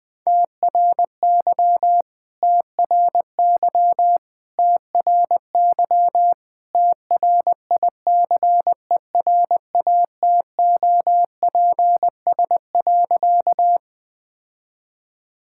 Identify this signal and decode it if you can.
Morse code, heard as a keyed tone, about 21 wpm, 700 Hz. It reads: TRY TRY TRY TRICERATOPS.
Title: All-Age Appropriate Jokes - 20wpm